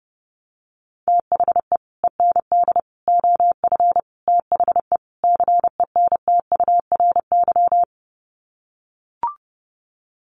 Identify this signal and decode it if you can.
Morse code, heard as a keyed tone, about 30 wpm, 700 Hz. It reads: THE END OF THE CENTURY
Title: Sets of 5 Words Encore2 30wpm